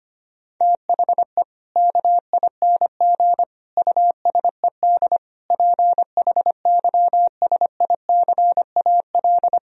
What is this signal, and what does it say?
Morse code, heard as a keyed tone, about 25 wpm, 700 Hz. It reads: THE KING USED PHYSICAL
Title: Sentences from Top 1000 Words - Review Part 4 - 25wpm